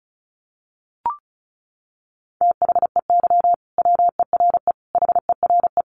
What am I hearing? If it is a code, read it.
Morse code, heard as a keyed tone, about 34 wpm, 700 Hz. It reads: THEY WERE HERE
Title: Sentences from Top 1200 Words 35wpm